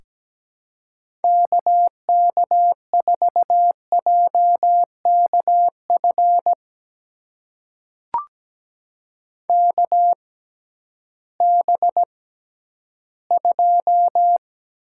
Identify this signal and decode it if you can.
Morse code, heard as a keyed tone, about 17 wpm, spaced as for 11 wpm, 700 Hz. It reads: KK4JKF KB2
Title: USA Call Signs - 6 Characters - 6x Character Spacing - 17wpm